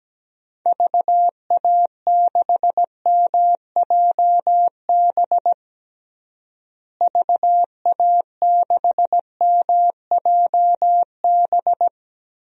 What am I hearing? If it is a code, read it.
Morse code, heard as a keyed tone, about 17 wpm, 700 Hz. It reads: VA6MJB VA6MJB
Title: DXCC-100 Call Signs - 6 Characters - Repeated 2x - 17wpm